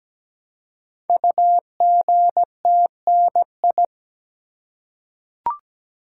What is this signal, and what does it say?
Morse code, heard as a keyed tone, about 17 wpm, 700 Hz. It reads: UGTNI
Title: Code Groups - 5 Letters - 17wpm